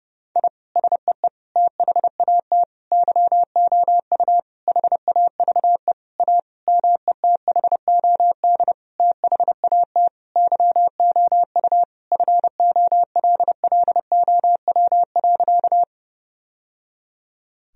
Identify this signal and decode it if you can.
Morse code, heard as a keyed tone, about 29 wpm, 700 Hz. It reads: I SEE THAT YOU HAVE A METHOD THAT YOU FOLLOW.
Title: Sentences from Top 1000 Words - Review Part 3 - 30wpm